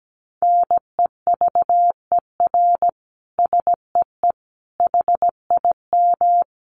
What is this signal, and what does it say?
Morse code, heard as a keyed tone, about 17 wpm, 700 Hz. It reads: NEVER SEE HIM
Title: Sentences from Top 300 Words 17wpm